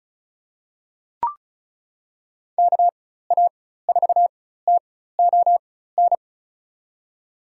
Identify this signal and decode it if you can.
Morse code, heard as a keyed tone, about 34 wpm, spaced as for 16 wpm, 700 Hz. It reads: KA4TON
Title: USA Call Signs - 6 Characters - 4x Character Spacing - 35wpm